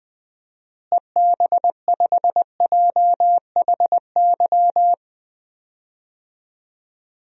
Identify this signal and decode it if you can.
Morse code, heard as a keyed tone, about 20 wpm, 700 Hz. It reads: EB5JHY
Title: DXCC-100 Call Signs - 6 Characters - Repeated 3x - 20wpm